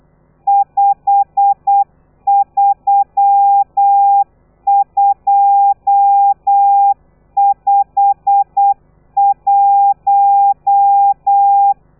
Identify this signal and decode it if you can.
Morse code, heard as a keyed tone, about 8 wpm, 800 Hz. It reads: 53251